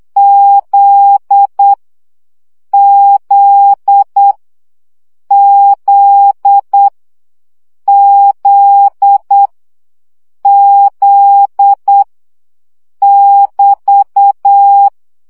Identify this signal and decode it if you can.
Morse code, heard as a keyed tone, about 8 wpm, 800 Hz. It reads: Z Z Z Z Z <BT>
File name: CW gyakorlas z 7 baud